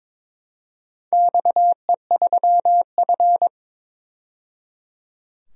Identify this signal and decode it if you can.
Morse code, heard as a keyed tone, about 22 wpm, 700 Hz. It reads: XE3F
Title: DXCC-100 Call Signs - 4 Characters - 22wpm